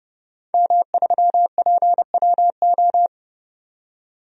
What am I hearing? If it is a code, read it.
Morse code, heard as a keyed tone, about 31 wpm, 700 Hz. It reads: M3PWO
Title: DXCC-100 Call Signs - 5 Characters - 30wpm